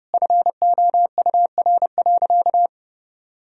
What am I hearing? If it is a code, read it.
Morse code, heard as a keyed tone, about 32 wpm, 700 Hz. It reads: FOUR.
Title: Sentences from Top 1000 Words - Review Part 1 - 30wpm